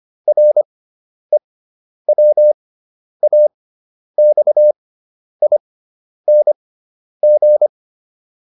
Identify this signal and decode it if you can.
Morse code, heard as a keyed tone, about 25 wpm, spaced as for 10 wpm, 600 Hz. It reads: REWAXING